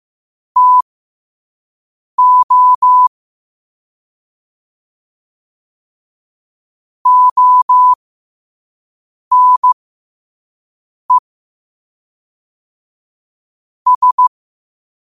Morse code, heard as a keyed tone, about 15 wpm, spaced as for 5 wpm, 1000 Hz. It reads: TO ONE S
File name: qotdfiles / 5wpm